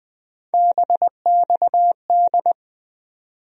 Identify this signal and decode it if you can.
Morse code, heard as a keyed tone, about 20 wpm, 700 Hz. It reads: BXD
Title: USA Call Sign Postfixes - 3 Letters 20wpm